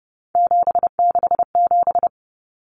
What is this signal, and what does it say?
Morse code, heard as a keyed tone, about 30 wpm, 700 Hz. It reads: 767
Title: Numbers - 3-Digit 30wpm